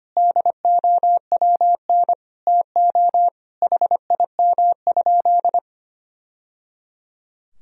Morse code, heard as a keyed tone, about 25 wpm, 700 Hz. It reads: DOWN TO HIM?
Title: Sentences from Top 1000 Words - Review Part 4 - 25wpm